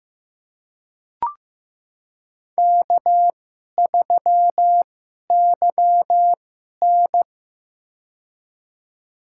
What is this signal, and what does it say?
Morse code, heard as a keyed tone, about 15 wpm, spaced as for 11 wpm, 700 Hz. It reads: K3YN
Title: USA Call Signs - 4 Characters - 2x Character Spacing - 15wpm